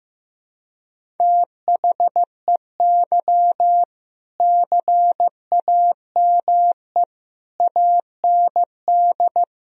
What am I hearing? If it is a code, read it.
Morse code, heard as a keyed tone, about 15 wpm, 700 Hz. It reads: THEY CAME AND
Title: Sentences from Top 1000 Words - Review Part 3 - 15wpm